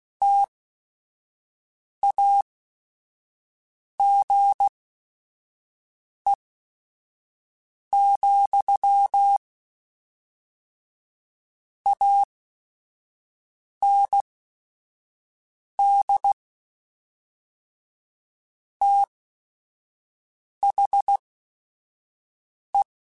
Morse code, heard as a keyed tone, about 16 wpm, spaced as for 5 wpm, 800 Hz. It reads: TAGE, AND THE
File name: Technical Subjects / Technical Subjects-mp3s / XL @5